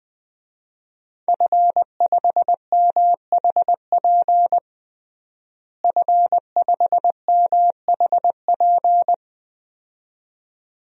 Morse code, heard as a keyed tone, about 20 wpm, 700 Hz. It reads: F5MHP F5MHP
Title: DXCC-100 Call Signs - 5 Characters - Repeated 2x - 20wpm